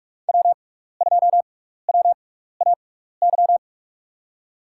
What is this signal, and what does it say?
Morse code, heard as a keyed tone, about 46 wpm, spaced as for 15 wpm, 700 Hz. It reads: W2WAY
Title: USA Call Signs - 5 Characters - 6x Character Spacing - 45wpm